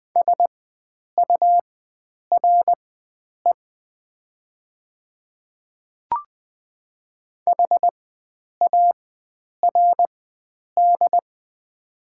Morse code, heard as a keyed tone, about 20 wpm, spaced as for 9 wpm, 690 Hz. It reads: SURE HARD
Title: 4x Character Spacing - Top 300 Words - 20wpm